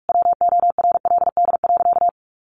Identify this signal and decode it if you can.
Morse code, heard as a keyed tone, about 44 wpm, 700 Hz. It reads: WORLD.
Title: Sentences from Top 1100 Words 45wpm